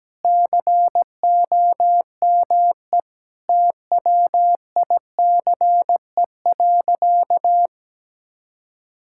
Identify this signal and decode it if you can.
Morse code, heard as a keyed tone, about 17 wpm, 700 Hz. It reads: COME TWICE.
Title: Sentences from Top 1600 Words 17wpm